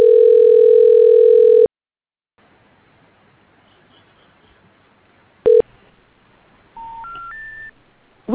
Ringback tone (North America), 440 + 480 Hz.